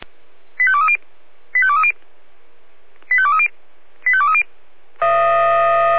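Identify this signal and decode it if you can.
Five-tone selective call (ZVEI), heard as quick runs of five tones, tones of about 70 ms. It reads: 85230 85230 85230 85230
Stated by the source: Five-tone ZVEI selective call, received on an SDR.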